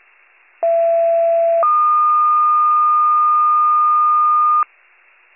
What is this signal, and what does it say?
Two-tone page, heard as a one-second tone, then a three-second tone, about 670 then 1220 Hz. It is heard from 0.6 to 4.6 s.